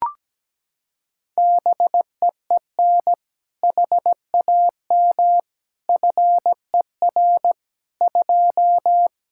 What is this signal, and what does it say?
Morse code, heard as a keyed tone, about 17 wpm, 700 Hz. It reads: BEEN HAM FER 2
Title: K1USN SST QSO Practice 17wpm